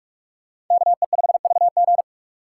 Morse code, heard as a keyed tone, about 48 wpm, 700 Hz. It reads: KE5VG